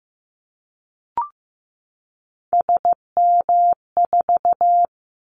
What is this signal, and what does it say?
Morse code, heard as a keyed tone, about 15 wpm, 700 Hz. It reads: SM4
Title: Callsign Prefixes for Top 25 DXCC - 15wpm